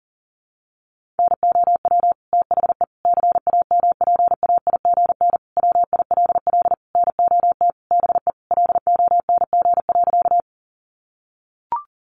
Morse code, heard as a keyed tone, about 40 wpm, 700 Hz. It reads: NOW THE CAMPAIGN WILL NOT BE LONG.
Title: Sentences from Top 1200 Words 40wpm